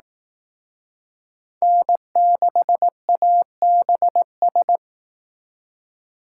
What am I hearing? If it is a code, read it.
Morse code, heard as a keyed tone, about 19 wpm, 700 Hz. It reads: N6ABS
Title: Speed Racing - USA 1x3 Call Signs 15wpm